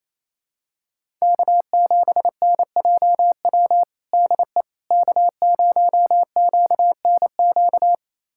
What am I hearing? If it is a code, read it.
Morse code, heard as a keyed tone, about 28 wpm, 700 Hz. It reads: K7NJW DE K0QNQ
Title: USA Callsign DE Callsign - 5 Characters - 28wpm